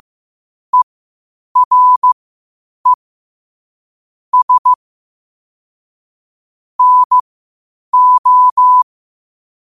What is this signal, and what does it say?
Morse code, heard as a keyed tone, about 15 wpm, spaced as for 8 wpm, 1000 Hz. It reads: ERE S NO
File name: qotdfiles / 8wpm